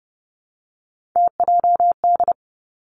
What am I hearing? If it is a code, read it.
Morse code, heard as a keyed tone, about 31 wpm, 700 Hz. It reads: TJD
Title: Three Challenging Letter-Number Combinations 30wpm